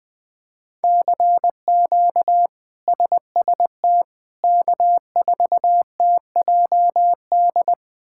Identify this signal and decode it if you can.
Morse code, heard as a keyed tone, about 20 wpm, 700 Hz. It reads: CQ SST K4TJD
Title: K1USN SST QSO Practice - 20wpm